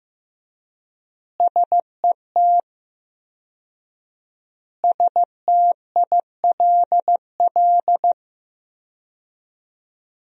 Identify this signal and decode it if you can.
Morse code, heard as a keyed tone, about 15 wpm, 700 Hz. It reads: SET STILL
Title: 4x Word-Spacing - Sentences from Top 300 Words - 15wpm